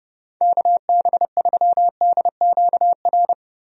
Morse code, heard as a keyed tone, about 30 wpm, 700 Hz. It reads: KB3DQR
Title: USA Call Signs - 6 Characters - Repeated 3x - 30wpm